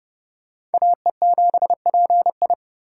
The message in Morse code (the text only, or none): AE7PI